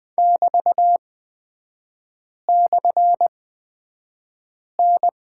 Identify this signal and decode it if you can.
Morse code, heard as a keyed tone, about 20 wpm, spaced as for 5 wpm, 700 Hz. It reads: <BT>/N
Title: 224 Lesson 36e - BT - Three Characters 20wpm